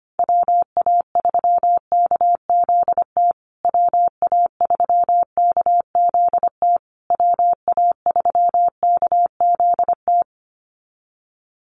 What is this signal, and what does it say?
Morse code, heard as a keyed tone, about 25 wpm, 700 Hz. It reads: WA3KZT WA3KZT WA3KZT
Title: USA Call Signs Repeated 3x 25wpm